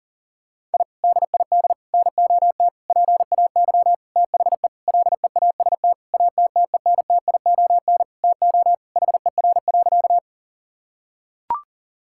Morse code, heard as a keyed tone, about 40 wpm, 700 Hz. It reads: I DID NOT PAY THE LEAST ATTENTION TO HER.
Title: Sentences from Top 1000 Words - Review Part 6 - 40wpm